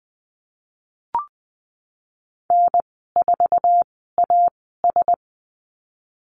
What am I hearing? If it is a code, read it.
Morse code, heard as a keyed tone, about 20 wpm, spaced as for 15 wpm, 700 Hz. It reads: N4AS
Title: 048 Lesson 9g - Number 4 - Call Signs 20wpm